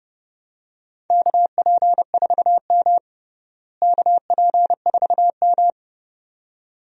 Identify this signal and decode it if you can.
Morse code, heard as a keyed tone, about 30 wpm, 700 Hz. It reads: KP4M KP4M
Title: DXCC-100 Call Signs - 4 Characters - Repeated 2x - 30wpm